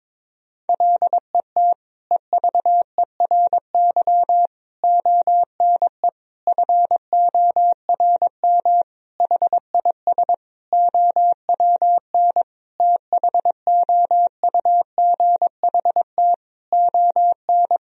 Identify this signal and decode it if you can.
Morse code, heard as a keyed tone, about 22 wpm, 700 Hz. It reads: LET EVERY ONE FORM HIS OWN THOUGHT ON
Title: Sentences from Top 1000 Words - Review Part 1 - 22wpm